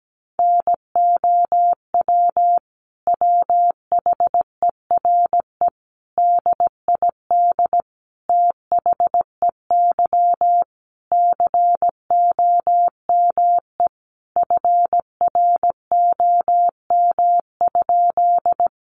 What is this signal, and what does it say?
Morse code, heard as a keyed tone, about 17 wpm, 700 Hz. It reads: NOW WHERE DID THEY COME FROM?
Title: Sentences from Top 200 Words 17wpm